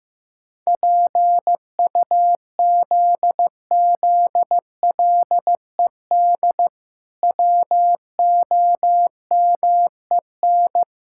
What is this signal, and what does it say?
Morse code, heard as a keyed tone, about 14 wpm, 700 Hz. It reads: PUZZLED WOMEN
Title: Pangram Challenge v2 - 15wpm